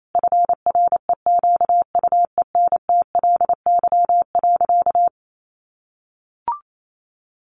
Morse code, heard as a keyed tone, about 28 wpm, 700 Hz. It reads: FREQUENTLY.
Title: Sentences from Top 1300 Words 28wpm